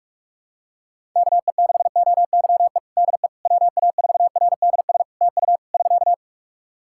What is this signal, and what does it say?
Morse code, heard as a keyed tone, about 46 wpm, 700 Hz. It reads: KE6OYE DE WA4RDS TU <SK>
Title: Practice QSO - Exchange 4 - Ending - 45wpm